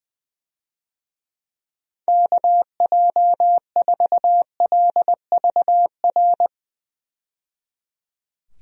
Morse code, heard as a keyed tone, about 20 wpm, 700 Hz. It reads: KJ4LVR
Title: USA Call Signs - 6 Characters - 20wpm